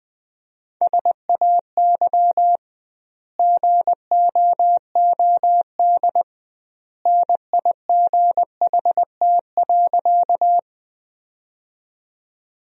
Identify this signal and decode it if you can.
Morse code, heard as a keyed tone, about 20 wpm, 700 Hz. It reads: SAY GOOD NIGHT.
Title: 2x Word-Spacing - Sentences from Top 300 Words - 20wpm